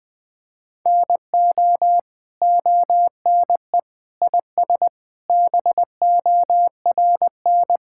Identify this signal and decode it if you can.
Morse code, heard as a keyed tone, about 20 wpm, 700 Hz. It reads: NO ONE IS BORN
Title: Sets of 9 Words - 20wpm